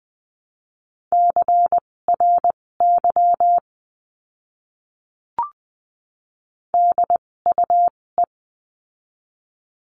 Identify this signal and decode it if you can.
Morse code, heard as a keyed tone, about 20 wpm, spaced as for 16 wpm, 700 Hz. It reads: CRY DUE
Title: 256 Lesson 42a - Words 12-20wpm